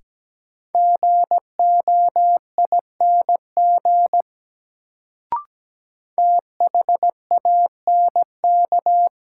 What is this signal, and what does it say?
Morse code, heard as a keyed tone, about 17 wpm, 700 Hz. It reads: GOING THANK
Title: Sets of 2 Words - New Word Drill - 17wpm